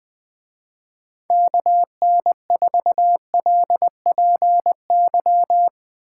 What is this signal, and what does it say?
Morse code, heard as a keyed tone, about 20 wpm, 700 Hz. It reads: KN4LPY